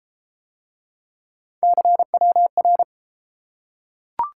Morse code, heard as a keyed tone, about 31 wpm, 700 Hz. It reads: CWR